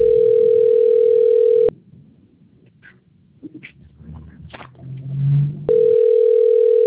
Ringback tone (North America), 440 + 480 Hz.